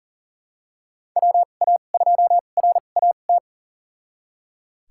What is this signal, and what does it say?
Morse code, heard as a keyed tone, about 40 wpm, spaced as for 29 wpm, 700 Hz. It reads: WA2RAT
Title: USA Call Signs - 6 Characters - 2x Character Spacing - 40wpm